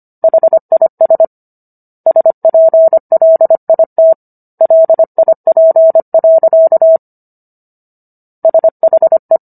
Morse code, heard as a keyed tone, about 25 wpm, 650 Hz. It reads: HIS SPLIT LIP. SHE
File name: Book One / 16-Chapter 16 - 25 WPM